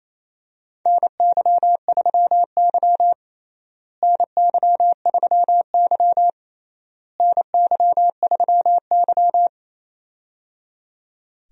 Morse code, heard as a keyed tone, about 27 wpm, 700 Hz. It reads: NY3Y NY3Y NY3Y
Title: USA Call Signs - 4 Characters - Repeated 3x - 28wpm